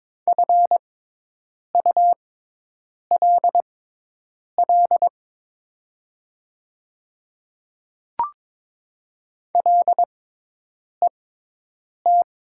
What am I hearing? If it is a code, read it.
Morse code, heard as a keyed tone, about 22 wpm, spaced as for 8 wpm, 700 Hz. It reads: FULL LET
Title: 6x Character Spacing - Top 500 Words - 22wpm